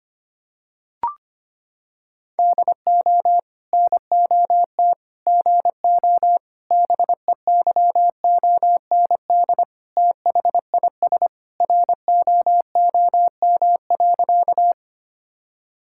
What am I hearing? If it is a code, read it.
Morse code, heard as a keyed tone, about 25 wpm, 700 Hz. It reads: DO NOT GO BEYOND THIS ROOM.
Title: Sentences from Top 1000 Words - Review Part 6 - 25wpm